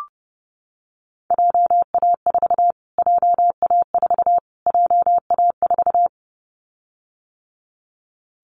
Morse code, heard as a keyed tone, about 30 wpm, 700 Hz. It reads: JA4 JA4 JA4
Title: Callsign Prefixes for Top 25 DXCC - Repeated 3x - 30wpm